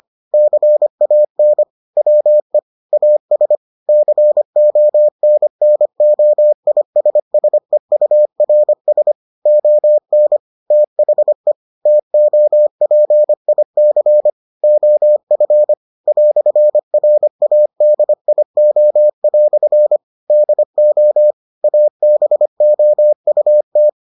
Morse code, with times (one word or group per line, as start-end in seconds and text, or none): 0.3-1.6 CAN
2.0-2.6 WE
2.9-3.6 AS
3.9-9.1 CONNOISSEURS
9.5-10.4 ON
10.7-11.5 THE
11.9-14.3 TOPIC
14.6-15.7 OF
16.1-20.0 "RADIO"
20.3-21.3 DO
21.6-23.9 ABOUT